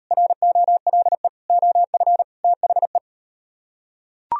Morse code, heard as a keyed tone, about 38 wpm, 700 Hz. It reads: ROLE OF THE